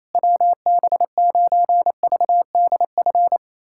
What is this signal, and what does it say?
Morse code, heard as a keyed tone, about 28 wpm, 700 Hz. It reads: WB9VDF
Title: USA Callsign DE Callsign - 6 Characters - 28wpm